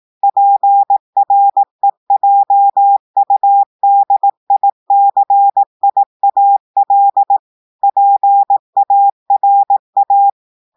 Morse code, heard as a keyed tone, about 18 wpm, 800 Hz. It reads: PREJUDICIAL PARA
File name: audios / portugais / portugais texte en clair 1 vitesse 18 mots minutes